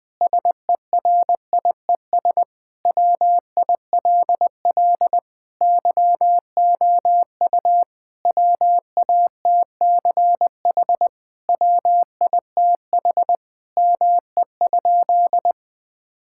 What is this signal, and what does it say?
Morse code, heard as a keyed tone, about 20 wpm, 700 Hz. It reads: SERIES WILL YOU WATCH WITH ME?
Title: Sentences from Top 1000 Words - Review Part 5 - 20wpm